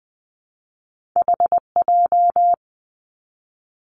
Morse code, heard as a keyed tone, about 20 wpm, 700 Hz. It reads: HJ
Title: Two Challenging Letter-Number Combinations 20wpm